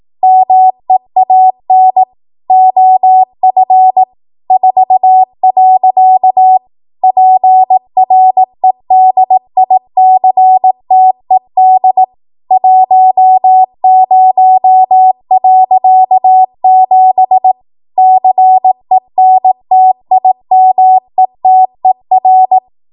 Morse code, heard as a keyed tone, about 18 wpm, 750 Hz. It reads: MEAN OF 4. PREDICTED 10.7 CENTIMETER
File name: Morse / 18WPM